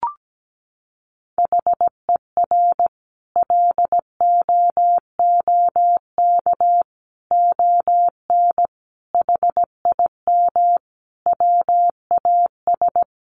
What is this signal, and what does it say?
Morse code, heard as a keyed tone, about 17 wpm, 700 Hz. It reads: HER LOOK ON HIM WAS
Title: Sentences from Top 1400 Words 17wpm